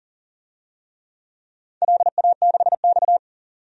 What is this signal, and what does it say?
Morse code, heard as a keyed tone, about 40 wpm, 700 Hz. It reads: LA6X
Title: DXCC-100 Call Signs - 4 Characters - 40wpm